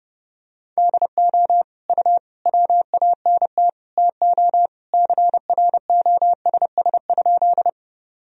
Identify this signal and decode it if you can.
Morse code, heard as a keyed tone, about 31 wpm, 700 Hz. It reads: DO U WANT TO CROSS?
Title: Sentences from Top 300 Words and Abbreviations 30wpm